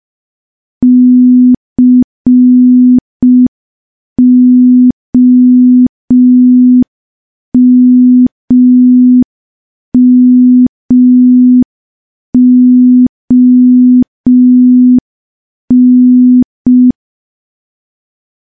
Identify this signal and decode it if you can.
Morse code, heard as a keyed tone, about 5 wpm, 250 Hz. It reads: COMMON
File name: morse / morse-training-05WPM